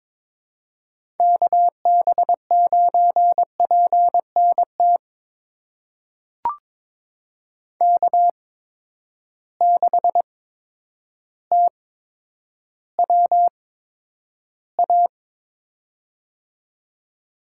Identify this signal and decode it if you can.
Morse code, heard as a keyed tone, about 22 wpm, spaced as for 10 wpm, 700 Hz. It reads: KB9PNT K6TWA
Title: USA Call Signs - 8x Character Spacing - 22wpm